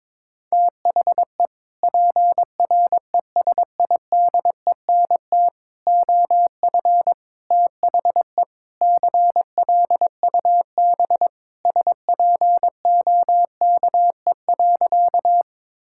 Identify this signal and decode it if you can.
Morse code, heard as a keyed tone, about 22 wpm, 700 Hz. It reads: THE PRESIDENT OF THE CLUB SPOKE.
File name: practice / 22wpm / Sentences with 4-Word Phrases - 22wpm